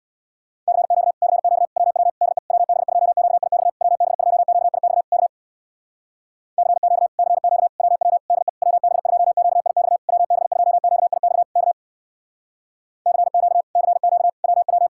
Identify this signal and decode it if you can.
Morse code, heard as a keyed tone, about 60 wpm, 700 Hz. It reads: CQ CQ WW DE WB2QIX WB2QIX K CQ CQ WW DE WB2QIX WB2QIX K CQ CQ WW
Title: Speed-Racing Common QSO Phrases Encore - 50wpm